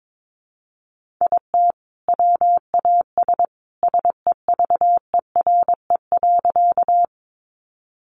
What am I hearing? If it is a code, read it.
Morse code, heard as a keyed tone, about 22 wpm, 700 Hz. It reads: IT WAS SEVERE.
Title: Sentences from Top 1400 Words 22wpm